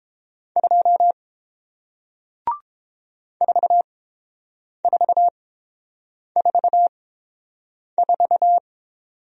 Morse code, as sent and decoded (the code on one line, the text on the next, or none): ..--- / ....- ....- ....- ....-
2 4444